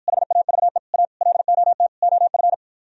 Morse code, heard as a keyed tone, about 52 wpm, 700 Hz. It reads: HAVE A LOT OF